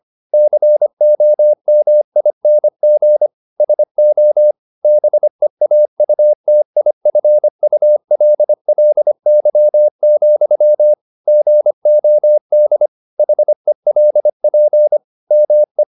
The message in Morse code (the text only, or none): COMING SO BEAUTIFULLY, GOD HELP ME